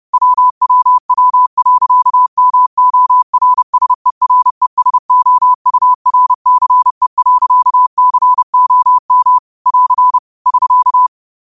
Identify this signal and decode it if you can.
Morse code, heard as a keyed tone, about 30 wpm, 1000 Hz. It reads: WWW.MORSERESOURCE.COM <AR> <SK>